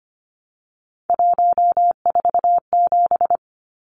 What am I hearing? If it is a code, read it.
Morse code, heard as a keyed tone, about 25 wpm, 700 Hz. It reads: 147
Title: Numbers - 3-Digit 25wpm